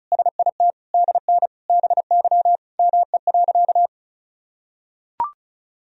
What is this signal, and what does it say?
Morse code, heard as a keyed tone, about 36 wpm, 700 Hz. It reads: SIT DN BY ME.
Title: Sentences from Top 500 Words and Abbreviations 35wpm